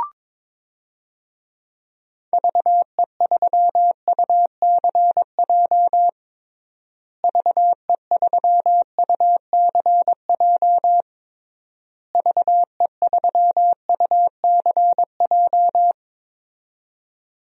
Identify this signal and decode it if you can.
Morse code, heard as a keyed tone, about 22 wpm, 700 Hz. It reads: VE3UCJ VE3UCJ VE3UCJ
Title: DXCC-100 Call Signs - 6 Characters - Repeated 3x - 22wpm